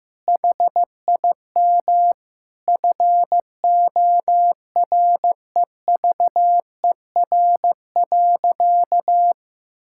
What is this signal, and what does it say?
Morse code, heard as a keyed tone, about 15 wpm, 700 Hz. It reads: HIM FOREVER.
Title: Sentences from Top 1600 Words 15wpm